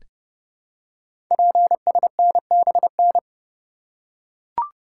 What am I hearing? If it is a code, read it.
Morse code, heard as a keyed tone, about 30 wpm, 700 Hz. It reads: PSNBN
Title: Code Groups - 5 Letters - 30wpm